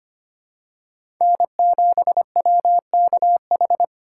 Morse code, heard as a keyed tone, about 25 wpm, 700 Hz. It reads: N7WKH
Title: USA Callsign DE Callsign - 5 Characters - 25wpm